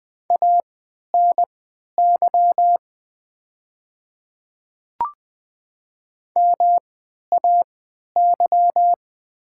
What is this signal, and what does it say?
Morse code, heard as a keyed tone, about 20 wpm, spaced as for 11 wpm, 700 Hz. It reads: ANY MAY
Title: 3x Character Spacing - Top 500 Words - 20wpm